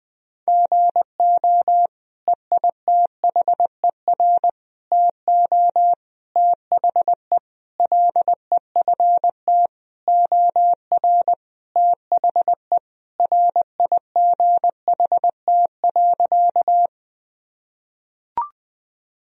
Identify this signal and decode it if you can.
Morse code, heard as a keyed tone, about 20 wpm, 700 Hz. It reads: GO EITHER TO THE LEFT OR THE RIGHT.
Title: Sentences from Top 1000 Words - Review Part 4 - 20wpm